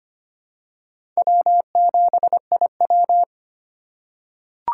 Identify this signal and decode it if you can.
Morse code, heard as a keyed tone, about 25 wpm, 700 Hz. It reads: W7IW
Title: USA Call Signs - 4 Characters - 3x Character Spacing - 25wpm